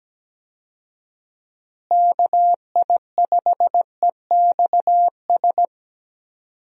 Morse code, heard as a keyed tone, about 17 wpm, 700 Hz. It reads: KI5EXS
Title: USA Call Signs - 6 Characters - Repeated 3x - 17wpm